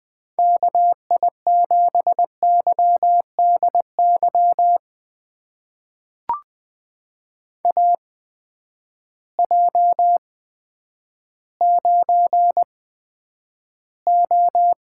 Morse code, heard as a keyed tone, about 21 wpm, spaced as for 10 wpm, 700 Hz. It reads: KI7YDY AJ9O
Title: USA Call Signs - 8x Character Spacing - 20wpm